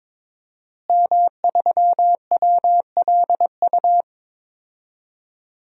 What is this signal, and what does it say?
Morse code, heard as a keyed tone, about 21 wpm, 700 Hz. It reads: M3WLU
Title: DXCC-100 Call Signs - 5 Characters - Repeated 3x - 22wpm